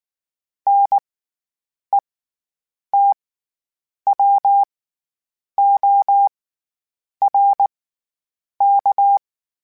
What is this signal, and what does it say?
Morse code, heard as a keyed tone, about 19 wpm, spaced as for 8 wpm, 800 Hz. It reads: NETWORK
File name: Special INGs / Special INGs-mp3s / SlowToFaster ing @18